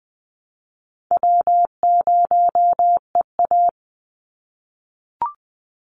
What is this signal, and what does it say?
Morse code, heard as a keyed tone, about 20 wpm, 700 Hz. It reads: W0EA
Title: Easy Call Signs - 20wpm